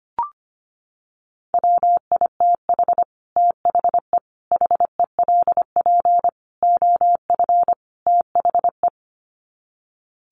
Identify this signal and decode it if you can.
Morse code, heard as a keyed tone, about 25 wpm, 700 Hz. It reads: WITH THE HELP OF THE
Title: Sets of 5 Words Encore 25wpm